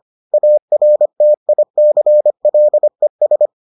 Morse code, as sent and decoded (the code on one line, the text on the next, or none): .- .-. - .. -.-. .-.. ...
ARTICLES